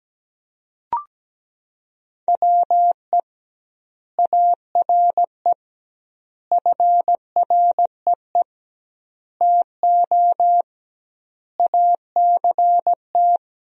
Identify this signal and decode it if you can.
Morse code, heard as a keyed tone, about 17 wpm, 700 Hz. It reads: WE ARE FREE TO ACT
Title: 2x Word-Spacing - Sentences from Top 500 Words - 17wpm